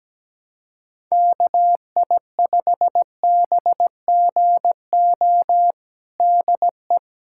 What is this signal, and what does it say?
Morse code, heard as a keyed tone, about 17 wpm, 700 Hz. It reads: KI5BGO DE
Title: USA Callsign DE Callsign 17wpm